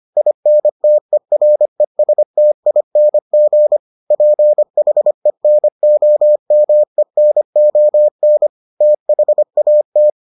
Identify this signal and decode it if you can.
Morse code, heard as a keyed tone, about 25 wpm, 600 Hz. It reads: INTERESTING PHENOMENON THAT